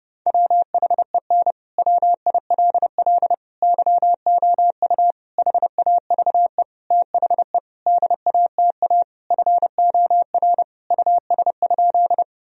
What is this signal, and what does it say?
Morse code, heard as a keyed tone, about 30 wpm, 700 Hz. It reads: WHEN WILL YOU HAVE THE DATA FOR US?
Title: Sentences from Top 1000 Words - Review Part 3 - 30wpm